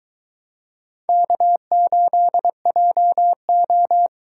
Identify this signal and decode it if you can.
Morse code, heard as a keyed tone, about 23 wpm, 700 Hz. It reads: K8JO